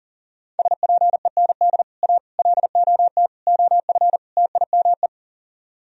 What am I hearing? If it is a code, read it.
Morse code, heard as a keyed tone, about 38 wpm, 700 Hz. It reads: SPEND A LOT OF TIME
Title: Speed-Racing - Sets of 5 Words 40wpm